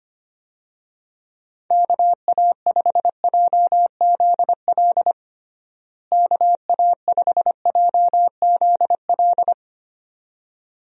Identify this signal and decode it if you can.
Morse code, heard as a keyed tone, about 25 wpm, 700 Hz. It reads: KA5JZL KA5JZL
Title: USA Call Signs - 6 Characters - Repeated 2x - 25wpm